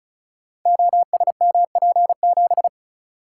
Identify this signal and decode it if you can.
Morse code, heard as a keyed tone, about 36 wpm, 690 Hz. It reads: OSMP7
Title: Code Groups - 5 Letter:Numbers - 35wpm